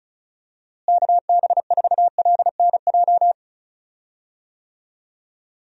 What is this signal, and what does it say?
Morse code, heard as a keyed tone, about 36 wpm, 700 Hz. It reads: KB4LNJ
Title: USA Call Signs - 6 Characters - Repeated 3x - 35wpm